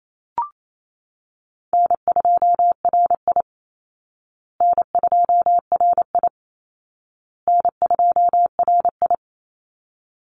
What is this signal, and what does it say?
Morse code, heard as a keyed tone, about 28 wpm, 700 Hz. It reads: N2RI N2RI N2RI
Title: Easy Call Signs Repeated 3x - 28wpm